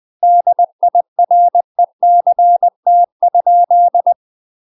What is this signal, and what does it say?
Morse code, heard as a keyed tone, about 20 wpm, 700 Hz. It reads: DIRECT?
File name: UK MASTER QSOs TRAINING / 20WPM / UK MASTER QSOs 0055